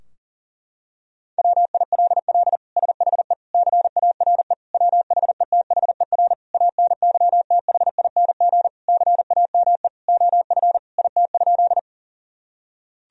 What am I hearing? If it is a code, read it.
Morse code, heard as a keyed tone, about 40 wpm, 700 Hz. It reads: WILL SHE CARE WHETHER ANYTHING CAME OF IT?
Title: Sentences from Top 1000 Words - Review Part 3 - 40wpm